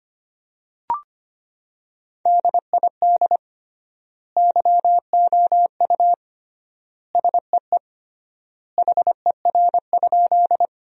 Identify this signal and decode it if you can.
Morse code, heard as a keyed tone, about 25 wpm, 700 Hz. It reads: DID YOU SEE HER?
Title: 3x Word-Spacing - Sentences from Top 100 Words - 25wpm